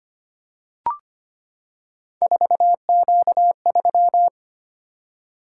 Morse code, heard as a keyed tone, about 25 wpm, 700 Hz. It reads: 4Q3
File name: Three Letter-Number Combinations - 25wpm